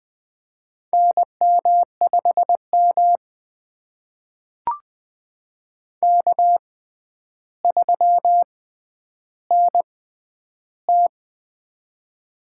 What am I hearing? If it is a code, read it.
Morse code, heard as a keyed tone, about 20 wpm, spaced as for 10 wpm, 700 Hz. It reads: NM5M K3NT
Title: USA Call Signs - 6x Character Spacing - 20wpm